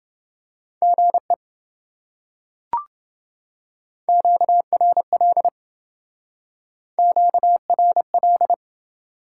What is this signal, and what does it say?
Morse code, heard as a keyed tone, about 27 wpm, 700 Hz. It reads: GE QRL QRL